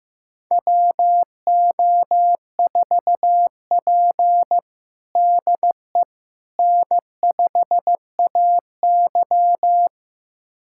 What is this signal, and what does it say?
Morse code, heard as a keyed tone, about 15 wpm, 700 Hz. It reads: WO4P DE N5AY
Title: USA Callsign DE Callsign - 4 Characters - 15wpm